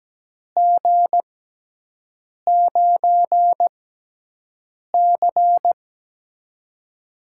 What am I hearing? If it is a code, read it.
Morse code, heard as a keyed tone, about 17 wpm, spaced as for 6 wpm, 700 Hz. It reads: G9C
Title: USA Call Signs - 4 Characters - 6x Character Spacing - 17wpm